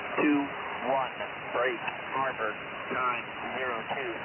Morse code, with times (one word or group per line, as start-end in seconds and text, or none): none